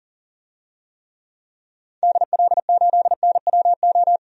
Morse code, heard as a keyed tone, about 40 wpm, 700 Hz. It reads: DL8NWO
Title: DXCC-100 Call Signs - 6 Characters - 40wpm